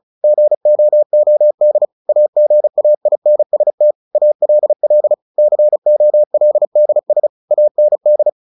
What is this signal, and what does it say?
Morse code, heard as a keyed tone, about 34 wpm, 600 Hz. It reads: GOOD AGAINST ALL COLDS AND
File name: the old man and the sea / the old man and the sea 1 35wpm